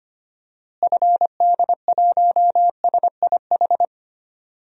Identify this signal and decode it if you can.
Morse code, heard as a keyed tone, about 25 wpm, 700 Hz. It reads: FD1SIH
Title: DXCC-100 Call Signs - 6 Characters - Repeated 3x - 25wpm